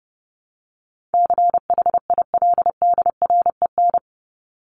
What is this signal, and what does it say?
Morse code, heard as a keyed tone, about 30 wpm, 700 Hz. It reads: CHILDREN
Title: Top 300 Words 30wpm